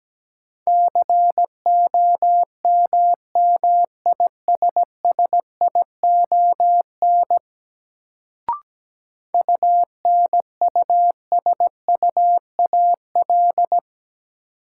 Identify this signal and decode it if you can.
Morse code, heard as a keyed tone, about 17 wpm, 700 Hz. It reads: COMMISSION UNUSUAL
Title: Sentences from Top 1400 Words - New Word Drill - 17wpm